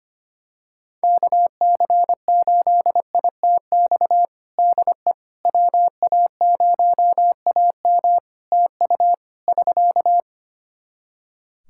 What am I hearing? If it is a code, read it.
Morse code, heard as a keyed tone, about 25 wpm, 700 Hz. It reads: KC8ITX DE WA0AM TU <SK>
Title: Practice QSO - Exchange 4 - Ending - 25wpm